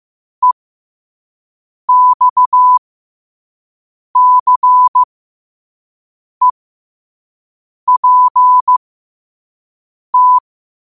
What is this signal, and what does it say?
Morse code, heard as a keyed tone, about 15 wpm, spaced as for 5 wpm, 1000 Hz. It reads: EXCEPT
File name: qotdfiles / 5wpm